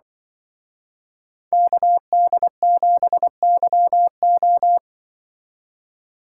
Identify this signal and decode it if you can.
Morse code, heard as a keyed tone, about 23 wpm, 700 Hz. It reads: KD7YO
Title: Speed Racing - USA 2x2 Call Signs 20wpm